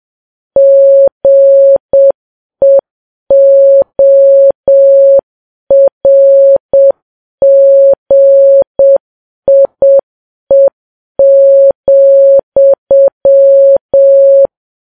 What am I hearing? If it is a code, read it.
Morse code, heard as a keyed tone, about 7 wpm, 550 Hz. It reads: GEORGIE,